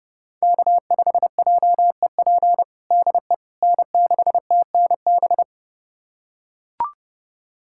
Morse code, heard as a keyed tone, about 32 wpm, 700 Hz. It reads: K5JEP DE N6TNB
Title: USA Callsign DE Callsign - 5 Characters - 30wpm